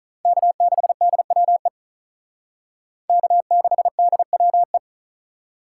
Morse code, heard as a keyed tone, about 44 wpm, 700 Hz. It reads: K6DWE K6DWE